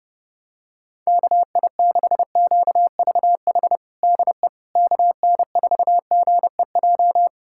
Morse code, heard as a keyed tone, about 32 wpm, 700 Hz. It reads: KI6QVH DE KN4GEJ
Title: USA Callsign DE Callsign 30wpm